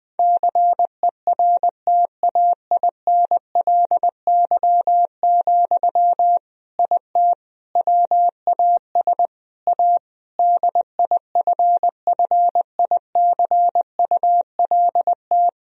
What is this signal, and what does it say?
Morse code, heard as a keyed tone, about 20 wpm, 700 Hz. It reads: CERTAINLY, IT WAS A DIFFICULT
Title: Sentences from Top 1000 Words - Review Part 2 - 20wpm